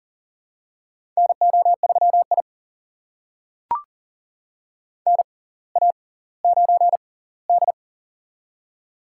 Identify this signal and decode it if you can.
Morse code, heard as a keyed tone, about 40 wpm, spaced as for 20 wpm, 700 Hz. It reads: NO3I NA9D
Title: USA Call Signs - 4 Characters - 6x Character Spacing - 40wpm